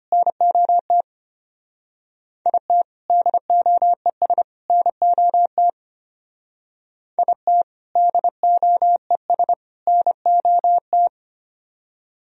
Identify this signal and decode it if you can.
Morse code, heard as a keyed tone, about 30 wpm, 700 Hz. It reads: NOT IT DOES NOT IT DOES NOT
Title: Speed-Racing - Sets of 3 Words 25wpm